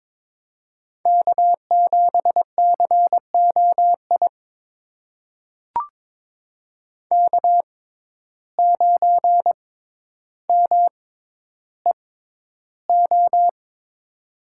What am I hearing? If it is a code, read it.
Morse code, heard as a keyed tone, about 22 wpm, spaced as for 11 wpm, 700 Hz. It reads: K7COI K9MEO
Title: USA Call Signs - 5 Characters - 6x Character Spacing - 22wpm